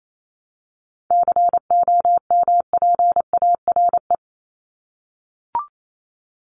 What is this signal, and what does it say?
Morse code, heard as a keyed tone, about 28 wpm, 700 Hz. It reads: COMPARE
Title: Top 1251-1500 Words 28wpm